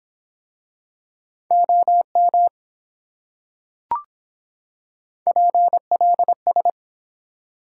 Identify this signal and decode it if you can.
Morse code, heard as a keyed tone, about 25 wpm, 700 Hz. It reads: OM PLS